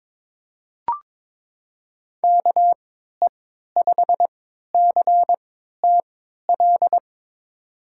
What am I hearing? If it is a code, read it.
Morse code, heard as a keyed tone, about 22 wpm, spaced as for 13 wpm, 700 Hz. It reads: KE5CTL
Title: USA Call Signs - 6 Characters - 3x Character Spacing - 22wpm